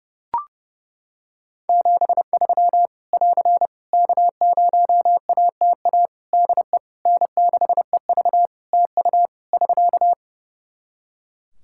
Morse code, heard as a keyed tone, about 32 wpm, 700 Hz. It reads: 73 <AR> K0ATA DE N6EV TU <SK>
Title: Practice QSO - Exchange 4 - Ending - 30wpm